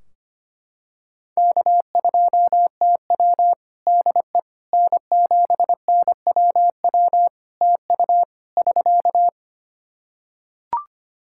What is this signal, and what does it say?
Morse code, heard as a keyed tone, about 25 wpm, 700 Hz. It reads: K2TW DE N7NWW TU <SK>
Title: K1USN SST QSO Practice 25wpm